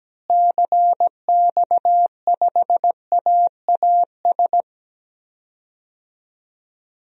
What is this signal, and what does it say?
Morse code, heard as a keyed tone, about 17 wpm, 700 Hz. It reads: CX5AAS
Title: DXCC-100 Call Signs - 6 Characters - Repeated 3x - 17wpm